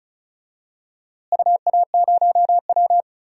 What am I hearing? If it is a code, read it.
Morse code, heard as a keyed tone, about 34 wpm, 700 Hz. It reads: UA0W
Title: DXCC-100 Call Signs - 4 Characters - Repeated 3x - 35wpm